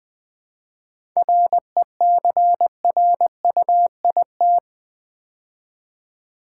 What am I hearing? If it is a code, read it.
Morse code, heard as a keyed tone, about 20 wpm, 700 Hz. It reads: RECRUIT
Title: 7 Letter Words - First 3 Letters Given - 20wpm